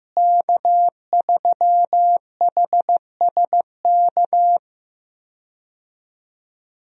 Morse code, heard as a keyed tone, about 16 wpm, 700 Hz. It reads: K3HSK
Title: USA Call Signs - 5 Characters - Repeated 2x - 15wpm